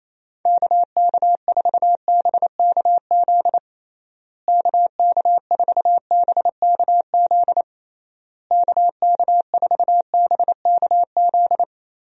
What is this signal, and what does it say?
Morse code, heard as a keyed tone, about 28 wpm, 700 Hz. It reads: KK4BKZ KK4BKZ KK4BKZ
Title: USA Call Signs - 6 Characters - Repeated 3x - 28wpm